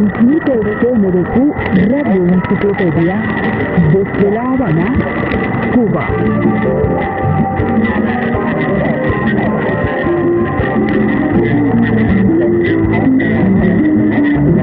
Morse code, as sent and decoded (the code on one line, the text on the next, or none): none